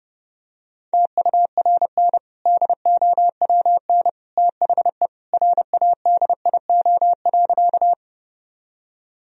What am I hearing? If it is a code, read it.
Morse code, heard as a keyed tone, about 30 wpm, 700 Hz. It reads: TURN DOWN THE RADIO.
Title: Sentences from Top 1000 Words - Review Part 3 - 30wpm